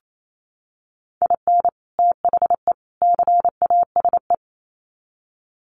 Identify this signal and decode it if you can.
Morse code, heard as a keyed tone, about 28 wpm, 700 Hz. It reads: IN THE CASE
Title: Sets of 3 Words 28wpm